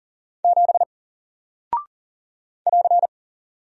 Morse code, heard as a keyed tone, about 40 wpm, 700 Hz. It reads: 7 <AR>
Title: Single Character (Letter:Number:Punctuation:Prosign) Alt 40wpm